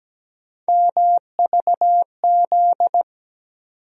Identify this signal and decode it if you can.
Morse code, heard as a keyed tone, about 17 wpm, 700 Hz. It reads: MVZ
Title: USA Call Sign Postfixes - 3 Letters 17wpm